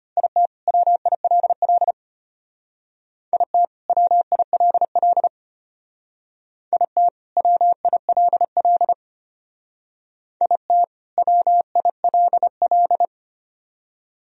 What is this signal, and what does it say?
Morse code, heard as a keyed tone, about 30 wpm, 700 Hz. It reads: IT WILL IT WILL IT WILL IT WILL
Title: Speed-Racing - Sets of 2 Words 25wpm